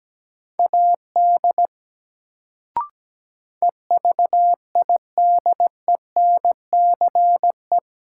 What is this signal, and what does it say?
Morse code, heard as a keyed tone, about 17 wpm, 700 Hz. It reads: AD EVIDENCE
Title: Sentences from Top 1300 Words - New Word Drill Alt - 17wpm